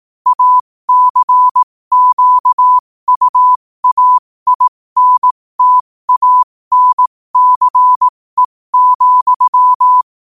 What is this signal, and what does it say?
Morse code, heard as a keyed tone, about 18 wpm, spaced as for 15 wpm, 1000 Hz. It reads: ACQUAINTANCE,
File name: qotdfiles / 15wpm